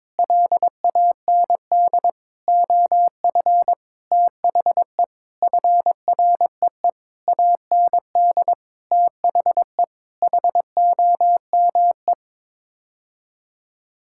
Morse code, heard as a keyed tone, about 22 wpm, 700 Hz. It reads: LAND OF THE FREE AND THE HOME
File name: practice / 22wpm / Sets of 8 Words - 22wpm